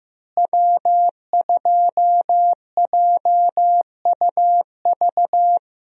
Morse code, heard as a keyed tone, about 16 wpm, 700 Hz. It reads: W2JUV
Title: USA Call Signs - 5 Characters - Repeated 2x - 15wpm